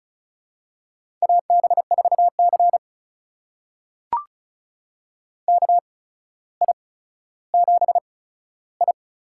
Morse code, heard as a keyed tone, about 36 wpm, spaced as for 15 wpm, 700 Hz. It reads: AB4C KI7I
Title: USA Call Signs - 8x Character Spacing - 35wpm